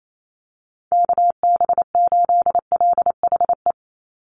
Morse code, heard as a keyed tone, about 28 wpm, 700 Hz. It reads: KB8LHE
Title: USA Call Signs 28wpm